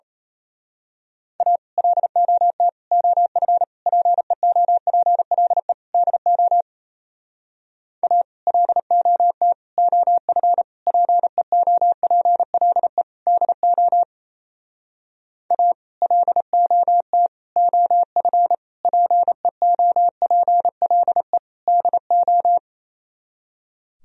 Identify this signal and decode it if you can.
Morse code, heard as a keyed tone, about 32 wpm, 700 Hz. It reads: A LOT OF PEOPLE DO A LOT OF PEOPLE DO A LOT OF PEOPLE DO
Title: Speed-Racing - Sets of 5 Words 28wpm